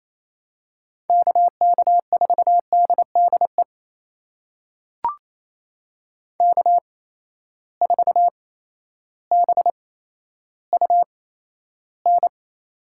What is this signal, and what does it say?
Morse code, heard as a keyed tone, about 28 wpm, spaced as for 13 wpm, 700 Hz. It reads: KK4DDE K4BUN
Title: USA Call Signs - 8x Character Spacing - 28wpm